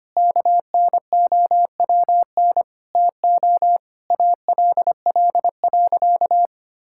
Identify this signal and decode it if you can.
Morse code, heard as a keyed tone, about 25 wpm, 700 Hz. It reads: KNOWN TO ALL.
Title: Sentences from Top 1000 Words - Review Part 5 - 25wpm